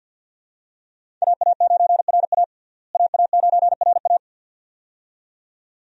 Morse code, heard as a keyed tone, about 52 wpm, 700 Hz. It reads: AA9RA AA9RA
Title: USA Call Signs - 5 Characters - Repeated 2x - 50wpm